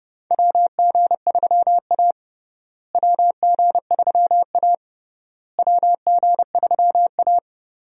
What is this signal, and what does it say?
Morse code, heard as a keyed tone, about 31 wpm, 700 Hz. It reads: WG3A WG3A WG3A
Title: USA Call Signs - 4 Characters - Repeated 3x - 30wpm